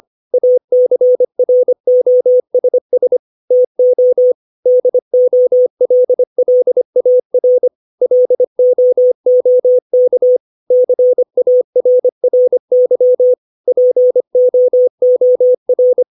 Morse code, heard as a keyed tone, about 25 wpm, 500 Hz. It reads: ACROSS TO DOLLAR LOOK CARRY POOR
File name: 632 words / 632 Words 25WPM